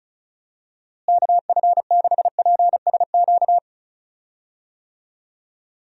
Morse code, heard as a keyed tone, about 34 wpm, 700 Hz. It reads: KF6PSQ
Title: USA Call Signs - 6 Characters - Repeated 2x - 35wpm